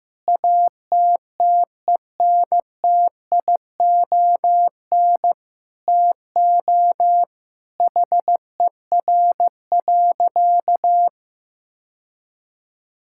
Morse code, heard as a keyed tone, about 15 wpm, 700 Hz. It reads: ATTENTION TO HER.
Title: Sentences from Top 1000 Words - Review Part 6 - 15wpm